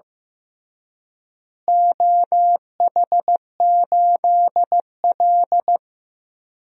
Morse code, heard as a keyed tone, about 15 wpm, 700 Hz. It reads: OH8L
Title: DXCC-100 Call Signs - 4 Characters - Repeated 3x - 15wpm